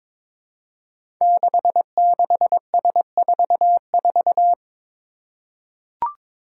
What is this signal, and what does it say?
Morse code, heard as a keyed tone, about 22 wpm, 700 Hz. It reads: 66S44
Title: Five Challenging Letter-Number Combinations 22wpm